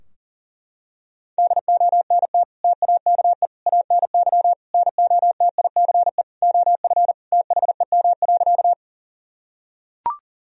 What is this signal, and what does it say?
Morse code, heard as a keyed tone, about 40 wpm, 700 Hz. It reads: DONT TAKE ANY NOTICE OF THEM.
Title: Sentences from Top 1000 Words - Review Part 2 - 40wpm